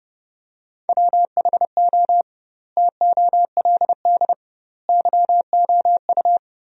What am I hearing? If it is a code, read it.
Morse code, heard as a keyed tone, about 30 wpm, 700 Hz. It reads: WHO TOLD YOU
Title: 2x Word-Spacing - Sentences from Top 400 Words - 30wpm